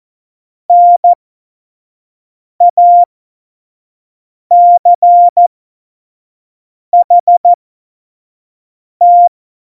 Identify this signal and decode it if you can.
Morse code, heard as a keyed tone, about 14 wpm, spaced as for 5 wpm, 700 Hz. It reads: NACHT
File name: Morsekurs / Witze / Witz 15. September 2020, 25 F70